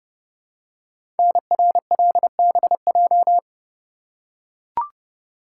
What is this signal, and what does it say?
Morse code, heard as a keyed tone, about 30 wpm, 700 Hz. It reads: NRLBJ
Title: Code Groups - 5 Letters - 30wpm